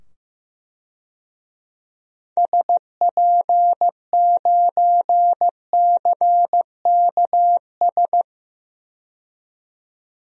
Morse code, heard as a keyed tone, about 15 wpm, 700 Hz. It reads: SP9CKS
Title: DXCC-100 Call Signs - 6 Characters - 15wpm